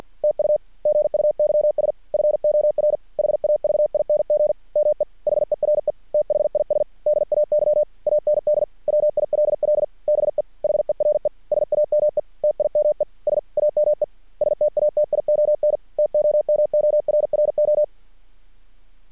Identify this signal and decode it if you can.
Morse code, heard as a keyed tone, about 50 wpm, 600 Hz. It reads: TU GUYS FOR HAVING ME HERE THIS DAY AND WILL BE HERE SAME TIME SAME STATION TOMORRO
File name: archived / CRSnet